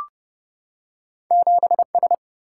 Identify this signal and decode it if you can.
Morse code, heard as a keyed tone, about 31 wpm, 700 Hz. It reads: 7S
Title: Single-Letter Single-Number 30wpm